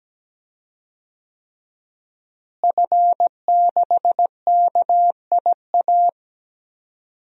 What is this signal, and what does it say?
Morse code, heard as a keyed tone, about 17 wpm, 700 Hz. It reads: F6KIA
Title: DXCC-100 Call Signs - 5 Characters - Repeated 3x - 17wpm